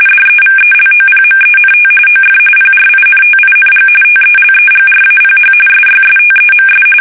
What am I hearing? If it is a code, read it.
RTTY, about 45.45 baud. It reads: DEVELOPMENT OF PRINTING TELEGRAPH SYSTEMS
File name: fsk850